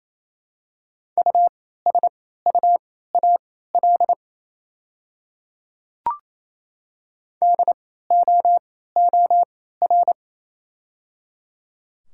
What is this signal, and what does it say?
Morse code, heard as a keyed tone, about 28 wpm, spaced as for 16 wpm, 700 Hz. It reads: USUAL DOOR
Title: 3x Character Spacing - Top 500 Words - 28wpm